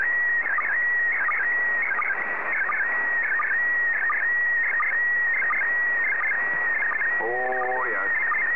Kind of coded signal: SSTV (slow-scan television)

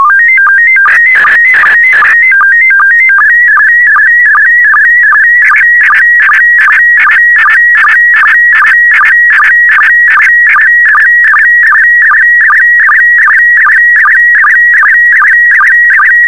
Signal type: SSTV (slow-scan television)

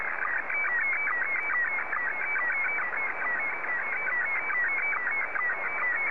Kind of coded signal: SSTV (slow-scan television)

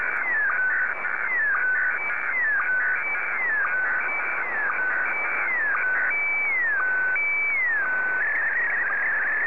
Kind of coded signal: SSTV (slow-scan television)

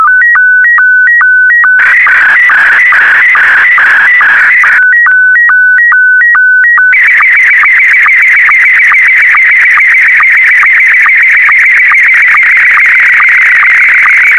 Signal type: SSTV (slow-scan television)

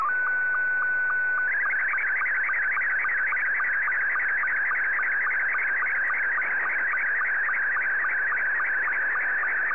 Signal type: SSTV (slow-scan television)

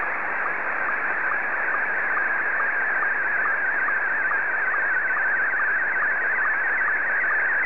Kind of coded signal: SSTV (slow-scan television)

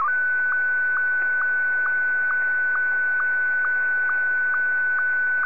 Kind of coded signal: SSTV (slow-scan television)

